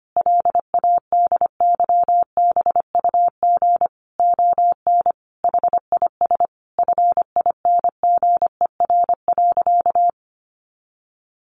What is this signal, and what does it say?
Morse code, heard as a keyed tone, about 25 wpm, 700 Hz. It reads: LADYBUG ON HIS FINGER.
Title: Additional Easy Phrases 25wpm